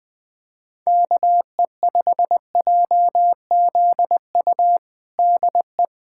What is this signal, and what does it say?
Morse code, heard as a keyed tone, about 21 wpm, 700 Hz. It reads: KE5JZU DE
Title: USA Callsign DE Callsign 20wpm